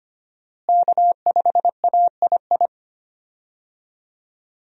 Morse code, heard as a keyed tone, about 25 wpm, 700 Hz. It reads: K5AII
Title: USA Call Signs - 5 Characters - Repeated 2x - 25wpm